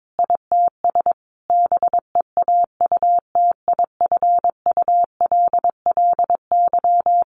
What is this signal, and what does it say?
Morse code, heard as a keyed tone, about 22 wpm, 700 Hz. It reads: ITS BEAUTIFULLY
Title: A Sound Of Thunder 22wpm